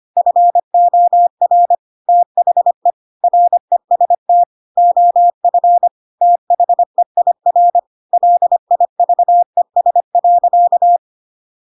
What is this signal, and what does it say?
Morse code, heard as a keyed tone, about 25 wpm, 700 Hz. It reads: FOR THE REST OF THEIR LIVES.
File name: Pinocchio / Pinocchio ch2 25 25 ik1hge